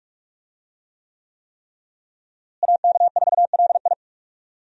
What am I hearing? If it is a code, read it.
Morse code, heard as a keyed tone, about 46 wpm, 700 Hz. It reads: AK4LI
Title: USA Call Signs - 5 Characters - Repeated 3x - 45wpm